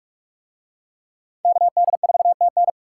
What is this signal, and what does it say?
Morse code, heard as a keyed tone, about 39 wpm, 700 Hz. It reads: KD4TN